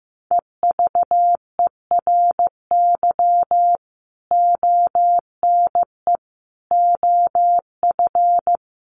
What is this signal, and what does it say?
Morse code, heard as a keyed tone, about 15 wpm, 700 Hz. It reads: EVERY ONE OF
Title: Sentences from Top 1500 Words 15wpm